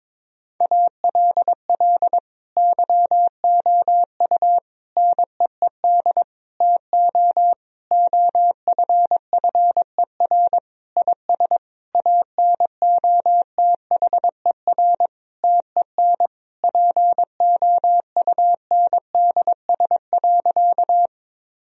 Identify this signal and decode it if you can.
Morse code, heard as a keyed tone, about 22 wpm, 700 Hz. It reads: ALL YOU NEED TO OFFER IS ANOTHER TEN POUNDS.
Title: Sentences from Top 1000 Words - Review Part 4 - 22wpm